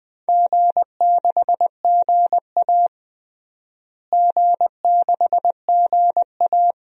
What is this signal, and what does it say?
Morse code, heard as a keyed tone, about 20 wpm, 700 Hz. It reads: G6GA G6GA
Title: DXCC-100 Call Signs - 4 Characters - Repeated 2x - 20wpm